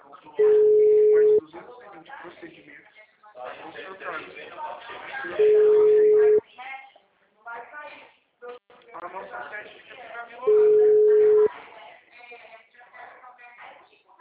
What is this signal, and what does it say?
Ringback tone (Europe, 425 Hz), with speech.